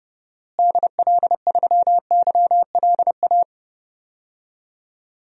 Morse code, heard as a keyed tone, about 30 wpm, 700 Hz. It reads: DL3YLA
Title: DXCC-100 Call Signs - 6 Characters - Repeated 2x - 30wpm